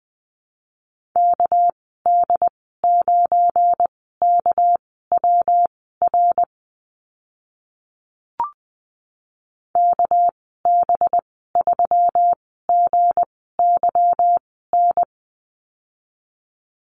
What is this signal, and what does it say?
Morse code, heard as a keyed tone, about 20 wpm, spaced as for 14 wpm, 700 Hz. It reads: KD9KWR KB3GYN
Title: 207 Lesson 33g - Letter j - Call Signs 20wpm